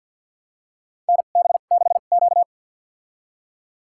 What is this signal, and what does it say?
Morse code, heard as a keyed tone, about 53 wpm, spaced as for 37 wpm, 700 Hz. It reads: NB6Q
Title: USA Call Signs - 2x Character Spacing - 50wpm